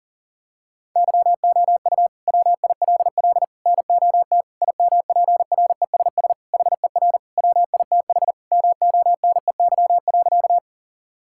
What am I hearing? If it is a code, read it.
Morse code, heard as a keyed tone, about 40 wpm, 700 Hz. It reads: YOU WILL NOT IMPRESS HER WITH MONEY.
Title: Sentences from Top 1600 Words 40wpm